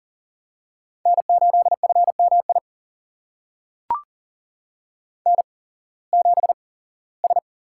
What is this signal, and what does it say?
Morse code, heard as a keyed tone, about 39 wpm, spaced as for 21 wpm, 700 Hz. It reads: N8FMI N7S
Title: USA Call Signs - 5 Characters - 8x Character Spacing - 40wpm